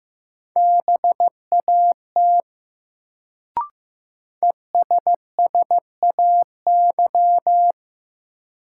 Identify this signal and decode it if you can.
Morse code, heard as a keyed tone, about 15 wpm, 700 Hz. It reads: BAT ESSAY
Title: Sentences from Top 1500 Words - New Word Drill - 15wpm